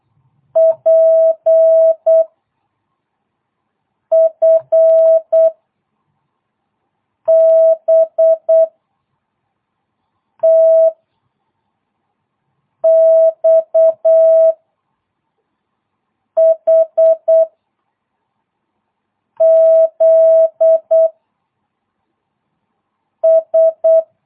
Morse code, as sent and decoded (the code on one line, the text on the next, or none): .--. / ..-. / -... / - / -..- / .... / --.. / ...
P F B T X H Z S